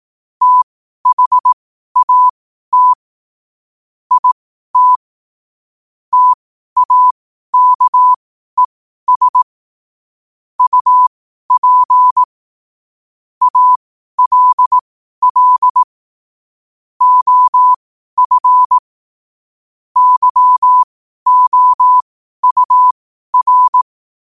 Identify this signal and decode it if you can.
Morse code, heard as a keyed tone, about 18 wpm, spaced as for 12 wpm, 1000 Hz. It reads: THAT IT TAKES UP ALL OF YOUR
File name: 2026-01-07-12wpm